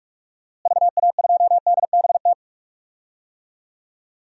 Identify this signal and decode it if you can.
Morse code, heard as a keyed tone, about 46 wpm, 700 Hz. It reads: VA2DBT
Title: DXCC-100 Call Signs - 6 Characters - Repeated 2x - 45wpm